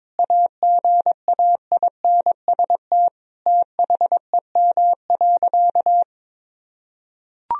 Morse code, heard as a keyed tone, about 22 wpm, 700 Hz. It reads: AGAINST THEM.
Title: Sentences from Top 1000 Words - Review Part 4 - 22wpm